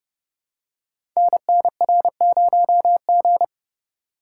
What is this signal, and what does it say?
Morse code, heard as a keyed tone, about 29 wpm, 700 Hz. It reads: NNR0G
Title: Code Groups - 5 Letter:Numbers - 30wpm